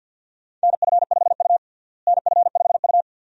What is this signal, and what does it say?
Morse code, heard as a keyed tone, about 52 wpm, 700 Hz. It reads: NF5U NF5U
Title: USA Call Signs - 4 Characters - Repeated 2x - 50wpm